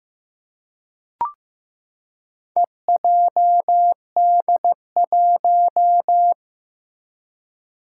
Morse code, heard as a keyed tone, about 15 wpm, 700 Hz. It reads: EJD1
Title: Four Challenging Letter-Number Combinations 15wpm